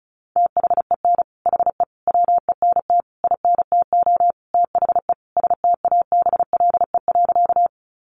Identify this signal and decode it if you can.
Morse code, heard as a keyed tone, about 34 wpm, 700 Hz. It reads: THEN HE WENT INTO THE STABLE.
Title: Sentences from Top 1500 Words 35wpm